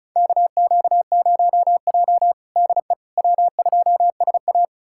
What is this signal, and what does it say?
Morse code, heard as a keyed tone, about 36 wpm, 700 Hz. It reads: KQ0J DE W2SA
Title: USA Callsign DE Callsign - 4 Characters - 35wpm